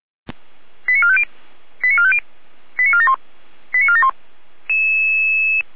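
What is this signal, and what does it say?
Five-tone selective call (ZVEI), heard as quick runs of five tones, tones of about 80 ms. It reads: 89450 89450 89471 89471
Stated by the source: ZVEI five-tone call received by SDR, from server of Feuerwehr Orlamünde fire brigade.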